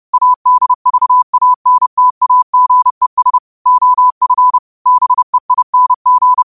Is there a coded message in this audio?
Yes: Morse code from 0.1 to 6.4 s, heard as a keyed tone, about 30 wpm, 1000 Hz. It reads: ADVANTAGES OF BEING